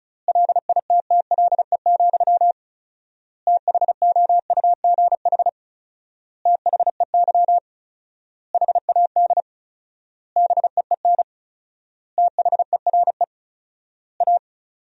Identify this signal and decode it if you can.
Morse code, heard as a keyed tone, about 34 wpm, 700 Hz. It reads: LITTLE, THOUGH THEY HAD BEEN THERE A
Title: 4x Word-Spacing - Sentences from Top 400 Words - 35wpm